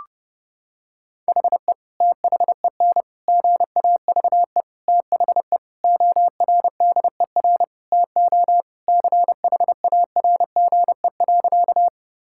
Morse code, heard as a keyed tone, about 29 wpm, 700 Hz. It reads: HE THEN GAVE THE ORDER TO CHARGE.
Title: Sentences from Top 1000 Words - Review Part 6 - 30wpm